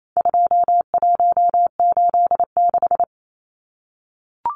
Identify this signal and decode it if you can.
Morse code, heard as a keyed tone, about 28 wpm, 700 Hz. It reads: 2186
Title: Numbers - 4-Digit 28wpm